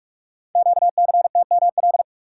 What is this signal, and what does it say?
Morse code, heard as a keyed tone, about 44 wpm, 700 Hz. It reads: QCTML
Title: Code Groups - 5 Letters - 45wpm